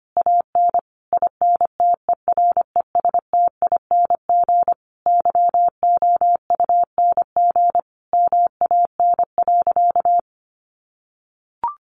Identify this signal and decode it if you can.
Morse code, heard as a keyed tone, about 25 wpm, 700 Hz. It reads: AN INTERESTING YOUNG MAN.
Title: Sentences from Top 1000 Words 25wpm